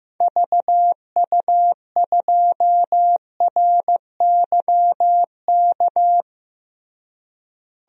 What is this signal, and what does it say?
Morse code, heard as a keyed tone, about 15 wpm, 700 Hz. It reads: VU2RYK
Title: DXCC-100 Call Signs - 6 Characters - Repeated 3x - 15wpm